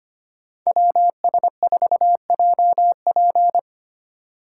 Morse code, heard as a keyed tone, about 25 wpm, 700 Hz. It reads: WS4JP
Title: USA Call Signs - 5 Characters - 3x Character Spacing - 25wpm